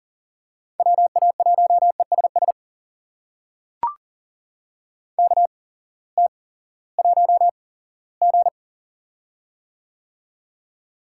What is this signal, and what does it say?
Morse code, heard as a keyed tone, about 40 wpm, spaced as for 20 wpm, 700 Hz. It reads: WA1ESS KT1G
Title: USA Call Signs - 8x Character Spacing - 40wpm